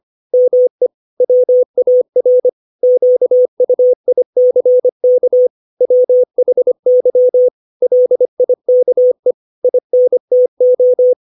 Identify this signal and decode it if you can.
Morse code, heard as a keyed tone, about 25 wpm, 500 Hz. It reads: ME WAR QUICK WHY LIKE INTO
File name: 632 words / 632 Words 25WPM